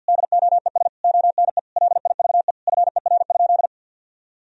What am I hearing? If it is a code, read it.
Morse code, heard as a keyed tone, about 52 wpm, 700 Hz. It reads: DOES ONE LIVE FER?
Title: Sentences from Top 200 Words and Abbreviations 50wpm